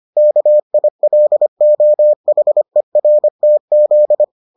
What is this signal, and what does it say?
Morse code, heard as a keyed tone, about 25 wpm, 600 Hz. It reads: KILOHERTZ